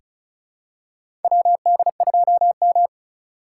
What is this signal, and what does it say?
Morse code, heard as a keyed tone, about 36 wpm, 700 Hz. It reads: WD2M
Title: USA Call Signs - 4 Characters - 4x Character Spacing - 35wpm